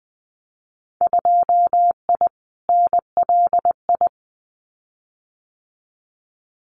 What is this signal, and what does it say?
Morse code, heard as a keyed tone, about 20 wpm, 700 Hz. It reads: 2I NLI
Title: Winter Field Day Exchanges 20wpm